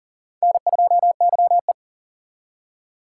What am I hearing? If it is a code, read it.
Morse code, heard as a keyed tone, about 52 wpm, 700 Hz. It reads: N2YE